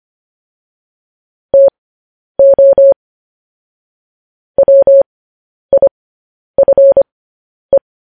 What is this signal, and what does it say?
Morse code, heard as a keyed tone, about 25 wpm, spaced as for 10 wpm, 550 Hz. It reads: TO WIFE